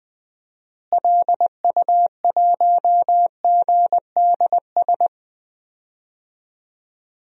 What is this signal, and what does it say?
Morse code, heard as a keyed tone, about 21 wpm, 700 Hz. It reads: LU1GDS
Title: DXCC-100 Call Signs - 6 Characters - Repeated 3x - 20wpm